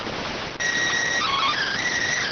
SSTV (slow-scan television), PD 290.